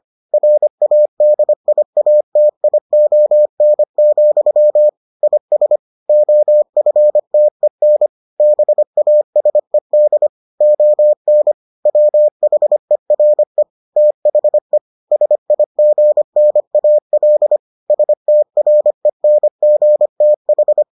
Morse code, heard as a keyed tone, about 25 wpm, 600 Hz. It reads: RADIATION, IS OFTEN BASED ON WHERE THE SIGNAL STRENGTH